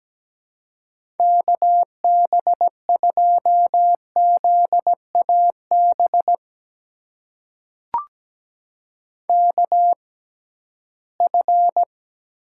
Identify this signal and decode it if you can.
Morse code, heard as a keyed tone, about 17 wpm, spaced as for 13 wpm, 700 Hz. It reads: KB2ZAB KF
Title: USA Call Signs - 6x Character Spacing - 17wpm